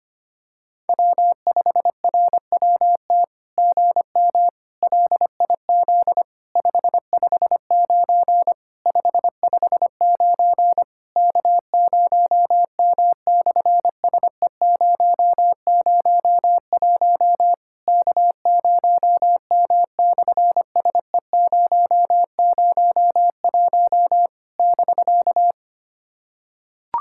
Morse code, heard as a keyed tone, about 24 wpm, 700 Hz. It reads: W5RWT GM LIZ 559 559 K0M/SE001 K0M/SE001 <BK>
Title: SOTA QSO Practice 25wpm